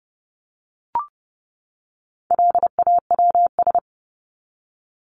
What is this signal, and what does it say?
Morse code, heard as a keyed tone, about 30 wpm, 700 Hz. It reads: LAWS
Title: ES:S Suffix - Top 100 Words with Base Words 30wpm